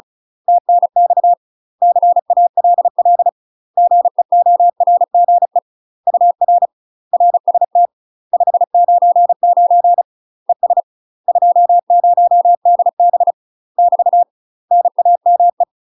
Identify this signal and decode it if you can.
Morse code, heard as a keyed tone, about 34 wpm, 700 Hz. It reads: TNX CALL GEORGE UR RST 599 ES 20DB <BT> NAME
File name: UK MASTER QSOs TRAINING / 35WPM / UK MASTER QSOs 0001